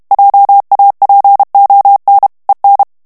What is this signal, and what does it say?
Morse code, heard as a keyed tone, about 32 wpm, 800 Hz. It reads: JAPON EN
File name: morse mp3 / FRANCAIS TEXTE CLAIR 1 VITESSE 30 MOTS MINUTES